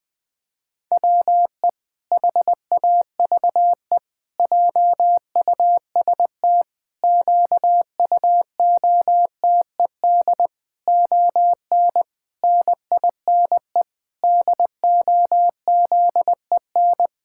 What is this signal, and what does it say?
Morse code, heard as a keyed tone, about 20 wpm, 700 Hz. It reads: WE HAVE JUST QUOTED ON NINE DOZEN
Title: Pangram Challenge v2 - 20wpm